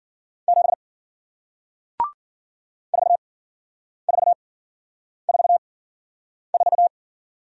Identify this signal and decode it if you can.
Morse code, heard as a keyed tone, about 51 wpm, spaced as for 9 wpm, 700 Hz. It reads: 7 4444